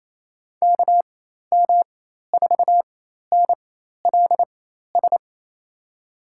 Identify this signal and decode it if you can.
Morse code, heard as a keyed tone, about 28 wpm, spaced as for 13 wpm, 700 Hz. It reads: KM4NLS
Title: USA Call Signs - 6 Characters - 4x Character Spacing - 28wpm